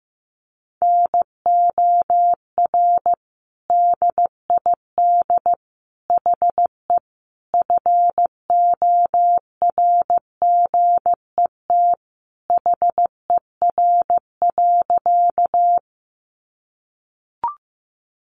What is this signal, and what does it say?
Morse code, heard as a keyed tone, about 15 wpm, 700 Hz. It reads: NOR DID HE FORGET HER.
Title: Sentences from Top 1500 Words 15wpm